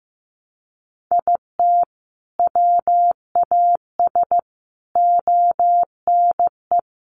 Morse code, heard as a keyed tone, about 15 wpm, 700 Hz. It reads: IT WAS ONE
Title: Sets of 5 Words 15wpm